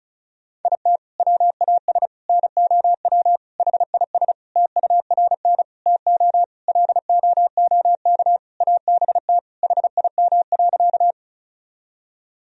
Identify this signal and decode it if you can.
Morse code, heard as a keyed tone, about 34 wpm, 700 Hz. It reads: IT WAS NOW HIS TURN TO LOOK ABT HIM.
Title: Sentences from Top 200 Words and Abbreviations 35wpm